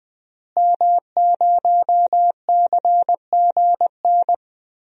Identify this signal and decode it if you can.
Morse code, heard as a keyed tone, about 20 wpm, 700 Hz. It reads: M0CGN
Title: DXCC-100 Call Signs - 5 Characters - Repeated 3x - 20wpm